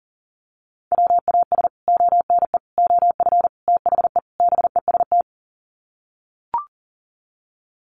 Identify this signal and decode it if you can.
Morse code, heard as a keyed tone, about 40 wpm, 700 Hz. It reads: WAS ONE OF THE BEST
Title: Sets of 5 Words Encore2 40wpm